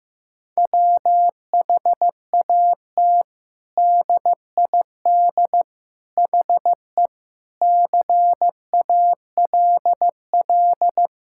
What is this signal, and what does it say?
Morse code, heard as a keyed tone, about 15 wpm, 700 Hz. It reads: WHAT DID HE CALL
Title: Sentences from Top 1600 Words 15wpm